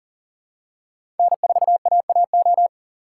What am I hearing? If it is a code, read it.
Morse code, heard as a keyed tone, about 40 wpm, 690 Hz. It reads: N4AAO
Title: USA Call Signs - 5 Characters - 3x Character Spacing - 40wpm